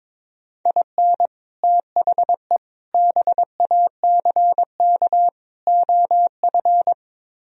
Morse code, heard as a keyed tone, about 22 wpm, 700 Hz. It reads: IN THE BACK OF
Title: Speed-Racing - Sets of 4 Words 22wpm